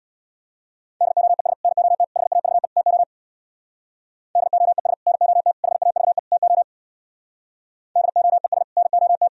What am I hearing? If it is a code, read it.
Morse code, heard as a keyed tone, about 64 wpm, 700 Hz. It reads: DOES NOT HAVE TO DOES NOT HAVE TO DOES NOT